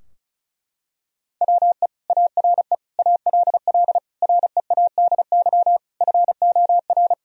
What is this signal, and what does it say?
Morse code, heard as a keyed tone, about 36 wpm, 700 Hz. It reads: WE ARE ALL READY FOR
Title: Sentences from Top 1000 Words - Review Part 4 - 35wpm